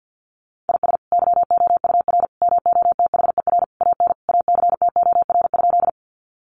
Morse code, heard as a keyed tone, about 50 wpm, 700 Hz. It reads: IS YOUR MOTHER AN ACTOR?
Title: Sentences from Top 1500 Words 50wpm